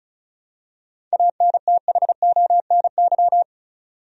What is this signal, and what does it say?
Morse code, heard as a keyed tone, about 36 wpm, 700 Hz. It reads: ANTHONY
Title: USA Common Names - 35wpm